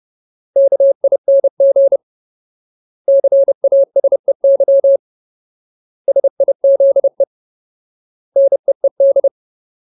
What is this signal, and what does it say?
Morse code, heard as a keyed tone, about 30 wpm, 550 Hz. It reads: KING CASEY SIZE NEED
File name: Ultimate Word List / 30WPM Ultimate